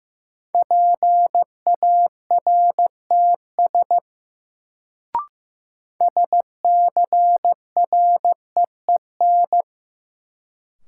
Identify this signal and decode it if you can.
Morse code, heard as a keyed tone, about 15 wpm, 700 Hz. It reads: PARTS SCREEN
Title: Sentences from Top 1000 Words - Review Part 6 - New Word Drill - 15wpm